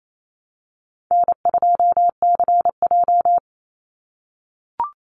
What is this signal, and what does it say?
Morse code, heard as a keyed tone, about 28 wpm, 700 Hz. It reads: N2CJ
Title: Easy Call Signs - 28wpm